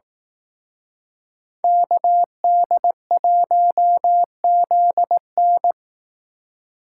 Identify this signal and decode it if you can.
Morse code, heard as a keyed tone, about 18 wpm, 700 Hz. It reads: KD1ZN
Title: Speed Racing - USA 2x2 Call Signs 15wpm